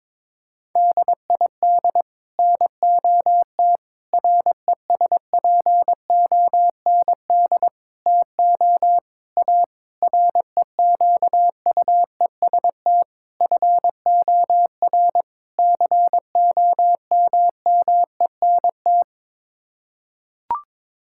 Morse code, heard as a keyed tone, about 22 wpm, 700 Hz. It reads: DID NOT RESPOND TO A REQUEST FOR COMMENT
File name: practice / 22wpm / Sets of 8 Words - 22wpm